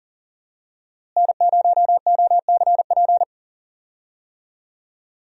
Morse code, heard as a keyed tone, about 40 wpm, 700 Hz. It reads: N0OCP
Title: USA Call Signs - 5 Characters - Repeated 3x - 40wpm